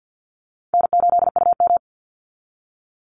Morse code, heard as a keyed tone, about 48 wpm, 700 Hz. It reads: N8UM
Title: Call Signs Repeated 2x - 50wpm